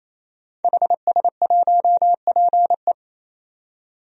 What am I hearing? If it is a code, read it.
Morse code, heard as a keyed tone, about 28 wpm, 700 Hz. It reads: HS1PE
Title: DXCC-100 Call Signs - 5 Characters - 28wpm